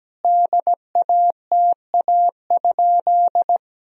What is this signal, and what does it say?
Morse code, heard as a keyed tone, about 17 wpm, 700 Hz. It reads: DATA?
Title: Sentences from Top 1000 Words - Review Part 3 - 17wpm